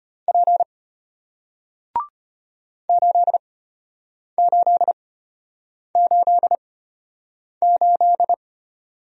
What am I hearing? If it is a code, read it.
Morse code, heard as a keyed tone, about 33 wpm, spaced as for 8 wpm, 700 Hz. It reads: P 8888